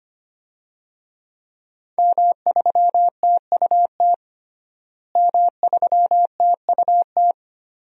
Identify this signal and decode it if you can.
Morse code, heard as a keyed tone, about 25 wpm, 700 Hz. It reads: M3TUT M3TUT
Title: DXCC-100 Call Signs - 5 Characters - Repeated 2x - 25wpm